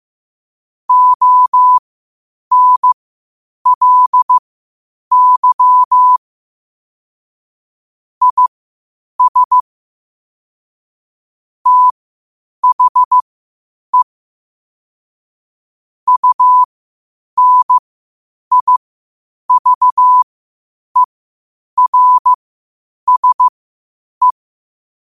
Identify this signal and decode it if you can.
Morse code, heard as a keyed tone, about 15 wpm, spaced as for 8 wpm, 1000 Hz. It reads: ONLY IS THE UNIVERSE
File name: qotdfiles / 8wpm